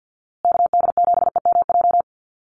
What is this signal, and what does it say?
Morse code, heard as a keyed tone, about 50 wpm, 700 Hz. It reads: KD7EMJ
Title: USA Call Signs Repeated 3x 50wpm